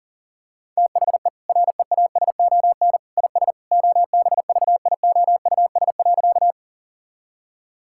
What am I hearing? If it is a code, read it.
Morse code, heard as a keyed tone, about 40 wpm, 700 Hz. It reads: THE REASON IS OBVIOUS.
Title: Sentences from Top 1600 Words 40wpm